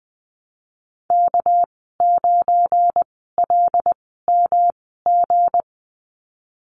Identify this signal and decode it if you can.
Morse code, heard as a keyed tone, about 20 wpm, spaced as for 15 wpm, 700 Hz. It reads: K9LMG
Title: 207 Lesson 33g - Letter j - Call Signs 20wpm